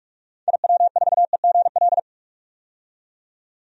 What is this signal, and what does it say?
Morse code, heard as a keyed tone, about 41 wpm, 700 Hz. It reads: IW4EGL
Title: DXCC-100 Call Signs - 6 Characters - 45wpm